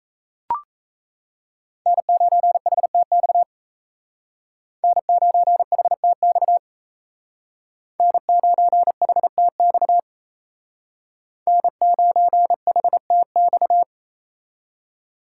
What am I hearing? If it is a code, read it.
Morse code, heard as a keyed tone, about 35 wpm, 700 Hz. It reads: N9HTX N9HTX N9HTX N9HTX